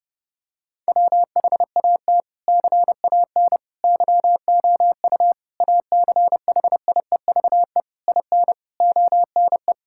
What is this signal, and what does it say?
Morse code, heard as a keyed tone, about 30 wpm, 700 Hz. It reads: WHAT CAN YOU ACHIEVE IN ONE
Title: Sentences from Top 1000 Words - Review Part 6 - 30wpm